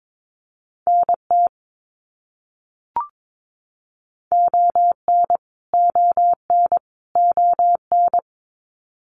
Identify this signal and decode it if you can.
Morse code, heard as a keyed tone, about 22 wpm, 700 Hz. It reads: NT ON ON ON
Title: Canadian Provinces and Territories - Repeated 3x - 22wpm